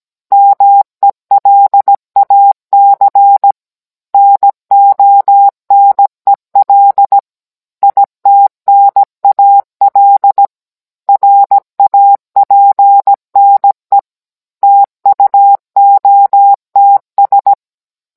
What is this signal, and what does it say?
Morse code, heard as a keyed tone, about 17 wpm, 800 Hz. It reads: MELAC NONEL ITNAL RAPNE TUOTS